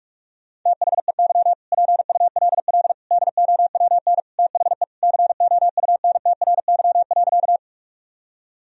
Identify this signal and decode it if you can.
Morse code, heard as a keyed tone, about 46 wpm, 700 Hz. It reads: THEY PULL DOWN THE COUNTRY.
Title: Sentences from Top 1000 Words - Review Part 2 - 45wpm